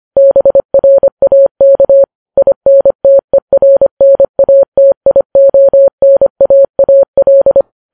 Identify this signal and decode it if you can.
Morse code, heard as a keyed tone, about 25 wpm, 550 Hz. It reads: BRAK INTERNATIONAAL